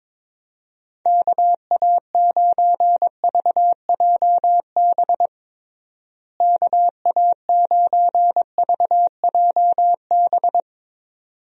Morse code, heard as a keyed tone, about 22 wpm, 700 Hz. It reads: KA9VJB KA9VJB
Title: USA Call Signs - 6 Characters - Repeated 2x - 22wpm